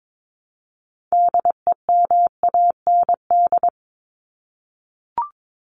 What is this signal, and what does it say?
Morse code, heard as a keyed tone, about 22 wpm, 700 Hz. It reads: DEMAND
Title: Top 501-750 Words 22wpm